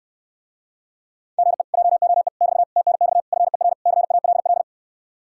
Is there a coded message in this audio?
Yes: Morse code from 1.4 to 4.6 s, heard as a keyed tone, about 67 wpm, 700 Hz. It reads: GE JOE <BT> TNX FER CALL